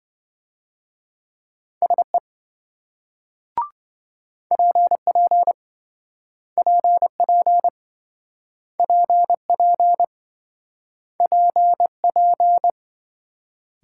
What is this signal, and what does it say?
Morse code, heard as a keyed tone, about 28 wpm, 700 Hz. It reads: SE PP PP PP PP